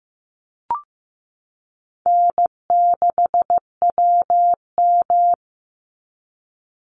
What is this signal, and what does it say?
Morse code, heard as a keyed tone, about 15 wpm, 700 Hz. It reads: N6WM
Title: Easy Call Signs Repeated 2x - 15wpm